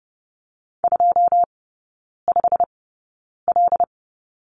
Morse code, heard as a keyed tone, about 30 wpm, spaced as for 9 wpm, 700 Hz. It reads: 25L
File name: Lesson 17e - Letter u - Three Characters 30wpm